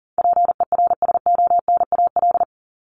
Morse code, heard as a keyed tone, about 40 wpm, 700 Hz. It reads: PERSONAL
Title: Top 501-750 Words 40wpm